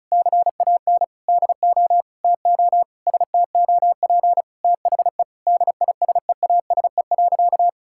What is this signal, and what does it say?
Morse code, heard as a keyed tone, about 34 wpm, 700 Hz. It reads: CAN DO TO STOP THE DISEASE.
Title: Sentences from Top 1000 Words - Review Part 6 - 35wpm